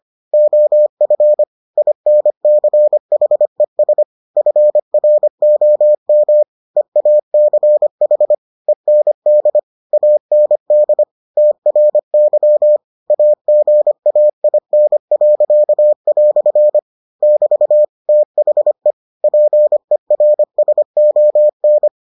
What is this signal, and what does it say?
Morse code, heard as a keyed tone, about 25 wpm, 600 Hz. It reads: OF INCHES FROM EACH END AND TRY AGAIN." <BT> THE PERSON